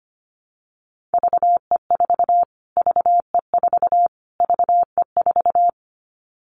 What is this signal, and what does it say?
Morse code, heard as a keyed tone, about 25 wpm, 700 Hz. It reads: VE4 VE4 VE4
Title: Callsign Prefixes for Top 25 DXCC - Repeated 3x - 25wpm